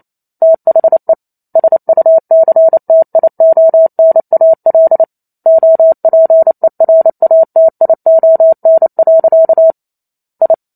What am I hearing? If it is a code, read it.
Morse code, heard as a keyed tone, about 28 wpm, 650 Hz. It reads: THE SUCTIONAL OPERATION. I
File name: Chapter 2 - 28 WPM